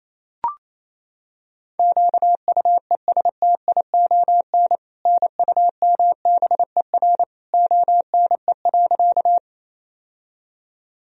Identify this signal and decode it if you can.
Morse code, heard as a keyed tone, about 28 wpm, 700 Hz. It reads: QUESTION NUMBER ONE.
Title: Sentences from Top 1000 Words - Review Part 1 - 28wpm